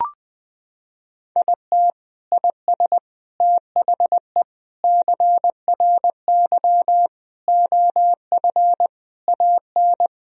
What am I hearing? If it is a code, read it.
Morse code, heard as a keyed tone, about 20 wpm, 700 Hz. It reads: IT IS THE CRY OF AN
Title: Sentences from Top 1000 Words - Review Part 2 - 20wpm